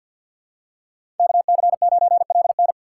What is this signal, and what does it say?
Morse code, heard as a keyed tone, about 52 wpm, 700 Hz. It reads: KC9LN
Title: USA Call Signs - 5 Characters - 6x Character Spacing - 50wpm